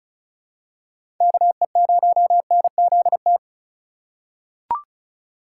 Morse code, heard as a keyed tone, about 36 wpm, 700 Hz. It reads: KE0NZT
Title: USA Call Signs - 6 Characters - 3x Character Spacing - 35wpm